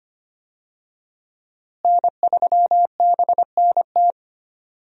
Morse code, heard as a keyed tone, about 25 wpm, 700 Hz. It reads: N3BNT
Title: USA Call Signs - 5 Characters - Repeated 2x - 25wpm